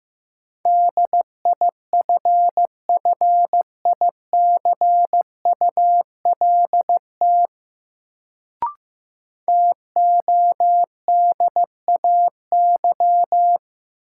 Morse code, heard as a keyed tone, about 15 wpm, 700 Hz. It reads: DIFFICULT TODAY
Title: Sentences from Top 600 Words - New Word Drill - 15wpm